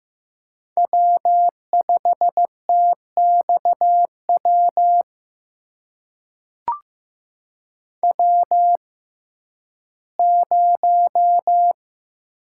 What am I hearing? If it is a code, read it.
Morse code, heard as a keyed tone, about 15 wpm, spaced as for 11 wpm, 700 Hz. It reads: W5TXW W0
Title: USA Call Signs - 5 Characters - 6x Character Spacing - 15wpm